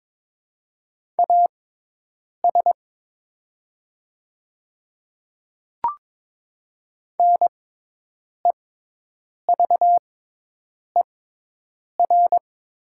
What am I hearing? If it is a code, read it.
Morse code, heard as a keyed tone, about 22 wpm, spaced as for 8 wpm, 700 Hz. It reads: AS NEVER
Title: 6x Character Spacing - Top 300 Words - 22wpm